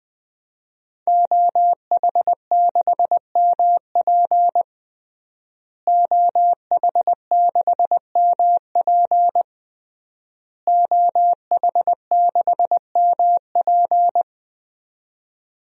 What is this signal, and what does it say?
Morse code, heard as a keyed tone, about 20 wpm, 700 Hz. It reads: OH6MP OH6MP OH6MP
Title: DXCC-100 Call Signs - 5 Characters - Repeated 3x - 20wpm